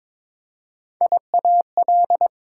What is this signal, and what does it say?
Morse code, heard as a keyed tone, about 21 wpm, 700 Hz. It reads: IAL
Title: Prefixes and Suffixes 22wpm